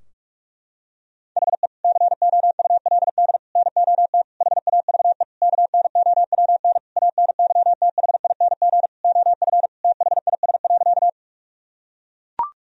Morse code, heard as a keyed tone, about 41 wpm, 700 Hz. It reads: HE COULD NOT HAVE KNOWN ANYTHING OF THIS.
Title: Sentences from Top 1000 Words - Review Part 3 - 45wpm